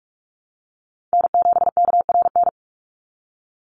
Morse code, heard as a keyed tone, about 44 wpm, 700 Hz. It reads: N7KRN
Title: NA SOTA Chaser Callsigns 45wpm